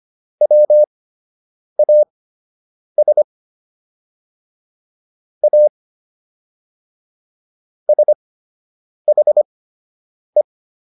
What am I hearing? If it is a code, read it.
Morse code, heard as a keyed tone, about 25 wpm, spaced as for 8 wpm, 600 Hz. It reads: WAS A SHE